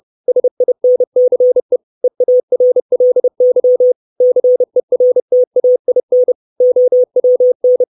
Morse code, heard as a keyed tone, about 30 wpm, 500 Hz. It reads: SINCE EARLY CERTAIN OWN
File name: 632 Words 30WPM